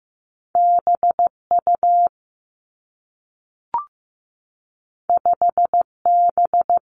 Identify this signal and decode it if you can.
Morse code, heard as a keyed tone, about 15 wpm, 700 Hz. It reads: B U 5 B
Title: Two Challenging Letter-Number Combinations 15wpm